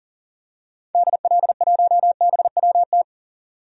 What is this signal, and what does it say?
Morse code, heard as a keyed tone, about 39 wpm, 700 Hz. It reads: DL1BWT
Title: DXCC-100 Call Signs - 6 Characters - 40wpm